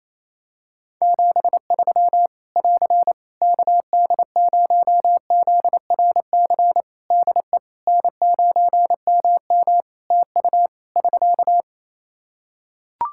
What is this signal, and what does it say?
Morse code, heard as a keyed tone, about 27 wpm, 700 Hz. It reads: 73 <AR> KD0ZRC DE N9MM TU <SK>
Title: Practice QSO - Exchange 4 - Ending - 28wpm